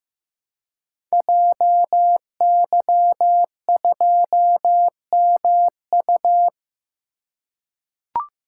Morse code, heard as a keyed tone, about 15 wpm, 700 Hz. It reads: JY2MU
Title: Code Groups - 5 Letter:Numbers - 15wpm